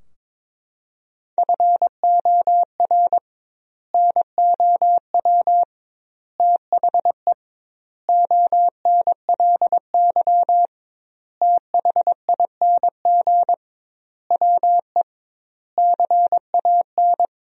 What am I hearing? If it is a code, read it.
Morse code, heard as a keyed tone, about 22 wpm, 700 Hz. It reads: FOR NOW THE ONLY THING WE CAN
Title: 2x Word-Spacing - Sentences from Top 200 Words - 22wpm